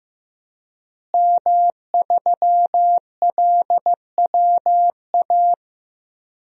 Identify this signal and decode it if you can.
Morse code, heard as a keyed tone, about 15 wpm, 700 Hz. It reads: M3LWA
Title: DXCC-100 Call Signs - 5 Characters - Repeated 3x - 15wpm